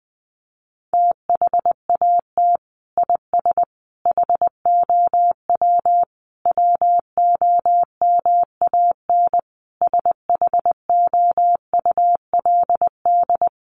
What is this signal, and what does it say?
Morse code, heard as a keyed tone, about 20 wpm, 700 Hz. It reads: THAT IS HOW WOMAN SHOULD
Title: Sentences from Top 1100 Words 20wpm